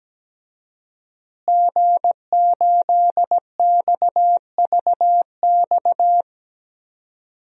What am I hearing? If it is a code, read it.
Morse code, heard as a keyed tone, about 17 wpm, 700 Hz. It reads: G8XVX
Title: DXCC-100 Call Signs - 5 Characters - Repeated 2x - 17wpm